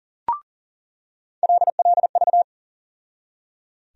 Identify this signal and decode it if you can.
Morse code, heard as a keyed tone, about 37 wpm, 700 Hz. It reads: LLV
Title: USA Call Sign Postfixes - 3 Letters 40wpm